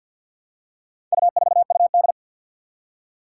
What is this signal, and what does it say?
Morse code, heard as a keyed tone, about 51 wpm, 700 Hz. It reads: U4UD
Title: Four Challenging Letter-Number Combinations 50wpm